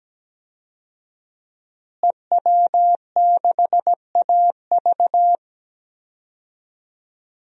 Morse code, heard as a keyed tone, about 17 wpm, 700 Hz. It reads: EW6AV
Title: DXCC-100 Call Signs - 5 Characters - Repeated 2x - 17wpm